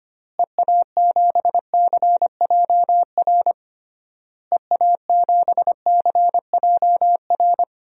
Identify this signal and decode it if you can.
Morse code, heard as a keyed tone, about 25 wpm, 700 Hz. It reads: EA7CJR EA7CJR
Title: DXCC-100 Call Signs - 6 Characters - Repeated 3x - 25wpm